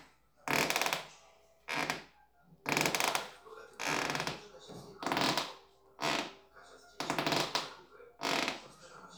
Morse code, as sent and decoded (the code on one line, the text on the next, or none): none